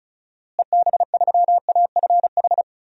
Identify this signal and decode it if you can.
Morse code, heard as a keyed tone, about 34 wpm, 700 Hz. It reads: EB3AFH
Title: DXCC-100 Call Signs - 6 Characters - Repeated 2x - 35wpm